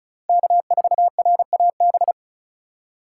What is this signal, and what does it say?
Morse code, heard as a keyed tone, about 36 wpm, 700 Hz. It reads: K4RAB
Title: K1USN SST QSO Practice - 35wpm